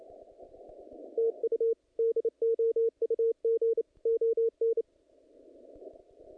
Morse code, heard as a keyed tone, about 28 wpm, 460 Hz. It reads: TU DOUG ON
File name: contests / naqp-cw-k5tr-op-k5pi- -2300